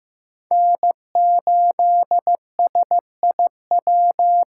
Morse code, heard as a keyed tone, about 15 wpm, 700 Hz. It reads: N8SIW
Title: USA Call Signs - 5 Characters - 2x Character Spacing - 15wpm